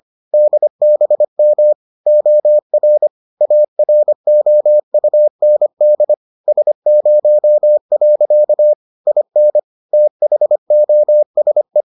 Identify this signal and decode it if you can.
Morse code, heard as a keyed tone, about 25 wpm, 600 Hz. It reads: DBM OR AROUND S0. IN THOSE